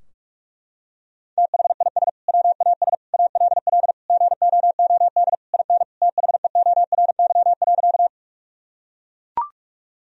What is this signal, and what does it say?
Morse code, heard as a keyed tone, about 42 wpm, 700 Hz. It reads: THIS WAS ALL GOOD IN THEORY.
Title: Sentences from Top 1000 Words - Review Part 3 - 45wpm